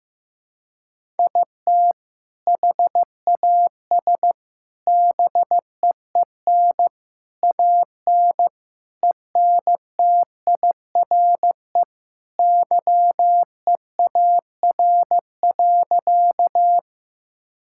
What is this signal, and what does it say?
Morse code, heard as a keyed tone, about 15 wpm, 700 Hz. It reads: IT HAS BEEN AN ENTIRE YEAR.
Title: Sentences from Top 1000 Words - Review Part 5 - 15wpm